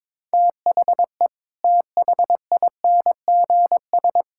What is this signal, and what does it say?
Morse code, heard as a keyed tone, about 22 wpm, 700 Hz. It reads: THE THINGS
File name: practice / 22wpm / Sentences with 3-Word Phrases - 22wpm